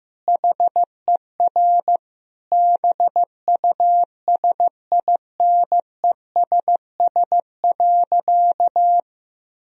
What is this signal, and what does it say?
Morse code, heard as a keyed tone, about 16 wpm, 700 Hz. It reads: HER BUSINESS.
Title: Sentences from Top 1000 Words - Review Part 1 - 15wpm